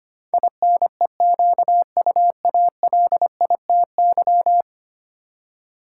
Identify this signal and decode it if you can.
Morse code, heard as a keyed tone, about 25 wpm, 700 Hz. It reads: INEQUALITY
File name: Ten Letter Words - 25wpm